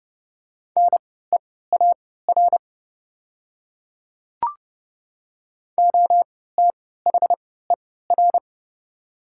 Morse code, heard as a keyed tone, about 30 wpm, spaced as for 17 wpm, 700 Hz. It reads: NEAR OTHER
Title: 3x Character Spacing - Top 300 Words - 30wpm